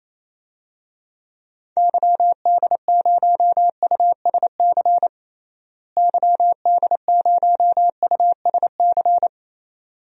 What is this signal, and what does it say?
Morse code, heard as a keyed tone, about 28 wpm, 700 Hz. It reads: YD0USC YD0USC
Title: DXCC-100 Call Signs - 6 Characters - Repeated 2x - 28wpm